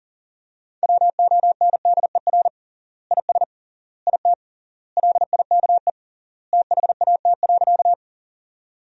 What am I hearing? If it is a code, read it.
Morse code, heard as a keyed tone, about 40 wpm, 700 Hz. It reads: WONDER IS IT LIKE THAT.
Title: 3x Word-Spacing - Sentences from Top 500 Words - 40wpm